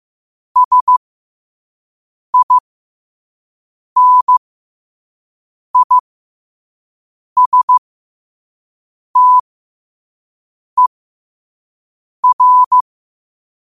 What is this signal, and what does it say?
Morse code, heard as a keyed tone, about 15 wpm, spaced as for 5 wpm, 1000 Hz. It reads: SINISTER
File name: qotdfiles / 5wpm